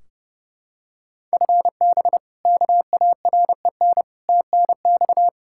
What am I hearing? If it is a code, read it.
Morse code, heard as a keyed tone, about 30 wpm, 700 Hz. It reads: FB KAREN TNX
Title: K1USN SST QSO Practice 30wpm